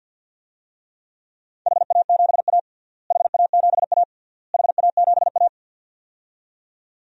Morse code, heard as a keyed tone, about 53 wpm, 700 Hz. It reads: HA7A HA7A HA7A
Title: DXCC-100 Call Signs - 4 Characters - Repeated 3x - 50wpm